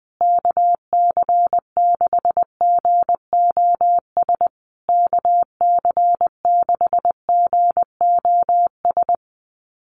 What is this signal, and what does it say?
Morse code, heard as a keyed tone, about 20 wpm, 700 Hz. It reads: KC6GOS KC6GOS
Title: USA Call Signs Repeated 3x 20wpm